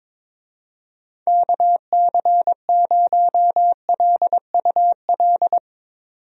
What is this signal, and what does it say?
Morse code, heard as a keyed tone, about 22 wpm, 700 Hz. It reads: KC0LUL
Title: USA Call Signs - 6 Characters - 22wpm